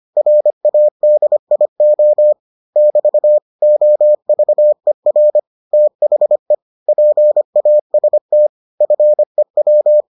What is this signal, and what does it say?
Morse code, heard as a keyed tone, about 25 wpm, 600 Hz. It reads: RADIO <BT> OVER THE PAST FEW